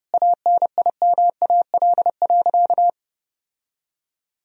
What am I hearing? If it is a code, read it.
Morse code, heard as a keyed tone, about 30 wpm, 700 Hz. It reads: ANIMAL.
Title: Sentences from Top 1000 Words - Review Part 1 - 30wpm